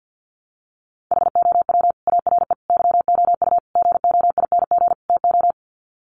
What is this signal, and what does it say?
Morse code, heard as a keyed tone, about 51 wpm, 700 Hz. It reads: HOW ARE YOU GOING TO
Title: Sets of 5 Words 50wpm